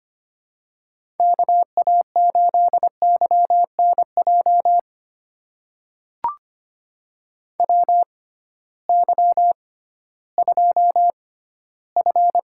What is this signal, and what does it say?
Morse code, heard as a keyed tone, about 25 wpm, spaced as for 15 wpm, 700 Hz. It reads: KA8YNJ WY2F
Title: USA Call Signs - 6x Character Spacing - 25wpm